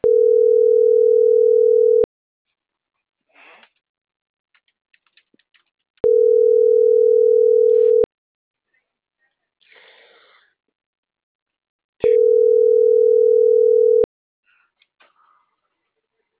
Ringback tone (North America), 440 + 480 Hz.